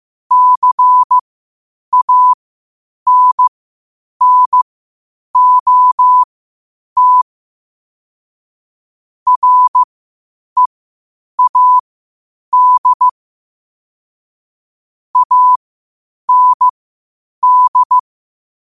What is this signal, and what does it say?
Morse code, heard as a keyed tone, about 15 wpm, spaced as for 8 wpm, 1000 Hz. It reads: CANNOT READ AND
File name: qotdfiles / 8wpm